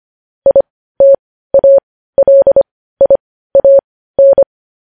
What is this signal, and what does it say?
Morse code, heard as a keyed tone, about 25 wpm, spaced as for 15 wpm, 550 Hz. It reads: ITALIAN
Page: news-headlines-in-morse-code-at-15-wpm